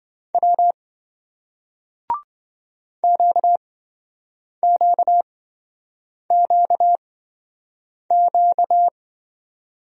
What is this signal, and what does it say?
Morse code, heard as a keyed tone, about 26 wpm, spaced as for 7 wpm, 700 Hz. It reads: W QQQQ